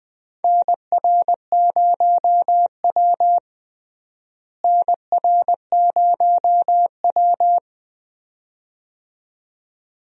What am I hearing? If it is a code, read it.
Morse code, heard as a keyed tone, about 21 wpm, 700 Hz. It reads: NR0W NR0W
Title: USA Call Signs - 4 Characters - Repeated 2x - 20wpm